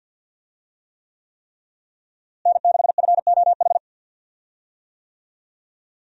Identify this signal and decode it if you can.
Morse code, heard as a keyed tone, about 47 wpm, 700 Hz. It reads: N6FOH
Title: USA Call Signs - 5 Characters - Repeated 3x - 50wpm